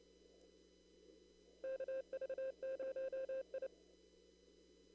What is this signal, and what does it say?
Morse code, heard as a keyed tone, about 28 wpm, 530 Hz. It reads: KV0I
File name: CW / kt5j wpx cw-op-k5pi